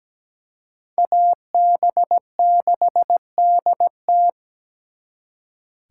Morse code, heard as a keyed tone, about 20 wpm, 700 Hz. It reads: AB6DT
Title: Speed Racing - USA 2x2 Call Signs 17wpm